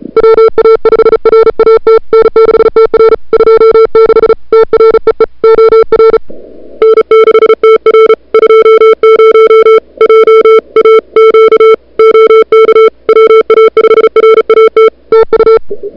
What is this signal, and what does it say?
Morse code, heard as a keyed tone, about 34 wpm, 440 Hz. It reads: WA5RAT N6TR 26 TREE OR N6TR 20 JAQ OK WA5RAT TU
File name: n6tr / Fab2007CWSprintHour1